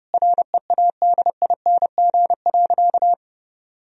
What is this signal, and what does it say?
Morse code, heard as a keyed tone, about 30 wpm, 700 Hz. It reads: READING.
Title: Sentences from Top 1000 Words - Review Part 3 - 30wpm